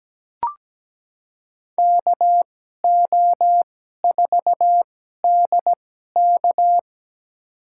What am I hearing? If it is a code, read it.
Morse code, heard as a keyed tone, about 17 wpm, spaced as for 12 wpm, 700 Hz. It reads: KO4DK
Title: USA Call Signs - 5 Characters - 2x Character Spacing - 17wpm